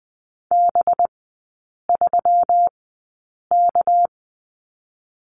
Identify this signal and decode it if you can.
Morse code, heard as a keyed tone, about 20 wpm, spaced as for 8 wpm, 700 Hz. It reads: B3K
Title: 198 Lesson 32e - Letter k - Three Characters 20wpm